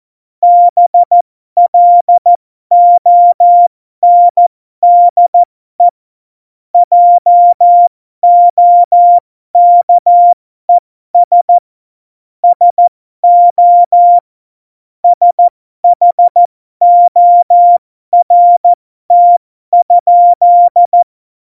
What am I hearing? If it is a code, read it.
Morse code, heard as a keyed tone, about 14 wpm, spaced as for 12 wpm, 700 Hz. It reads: BLONDE JOKES SO SHORT?
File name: Morsekurs / Witze